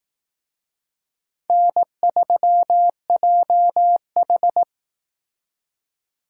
Morse code, heard as a keyed tone, about 18 wpm, 700 Hz. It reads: N3JH
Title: Speed Racing - USA Call Signs 15wpm